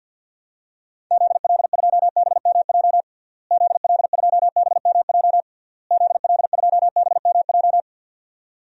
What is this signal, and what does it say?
Morse code, heard as a keyed tone, about 52 wpm, 700 Hz. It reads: ZL2BMJ ZL2BMJ ZL2BMJ
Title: DXCC-100 Call Signs - 6 Characters - Repeated 3x - 50wpm